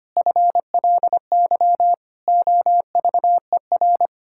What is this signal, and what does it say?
Morse code, heard as a keyed tone, about 25 wpm, 700 Hz. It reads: FLY OVER
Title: Pangram Challenge v2 - 25wpm